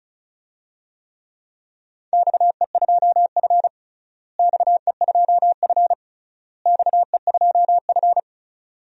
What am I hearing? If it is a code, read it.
Morse code, heard as a keyed tone, about 36 wpm, 700 Hz. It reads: XE2F XE2F XE2F
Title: DXCC-100 Call Signs - 4 Characters - Repeated 3x - 35wpm